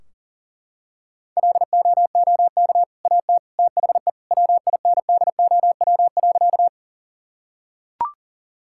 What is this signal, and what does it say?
Morse code, heard as a keyed tone, about 37 wpm, 700 Hz. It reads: LOOK AT THE WINDOW.
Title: Sentences from Top 1600 Words 40wpm